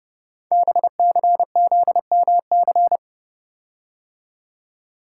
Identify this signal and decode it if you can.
Morse code, heard as a keyed tone, about 29 wpm, 700 Hz. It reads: BCZMC
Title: Code Groups - 5 Letters - 30wpm